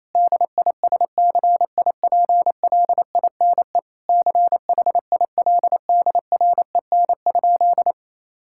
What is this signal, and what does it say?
Morse code, heard as a keyed tone, about 28 wpm, 700 Hz. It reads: DISCIPLINE CHILDREN?
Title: Sentences from Top 1000 Words - Review Part 6 - 28wpm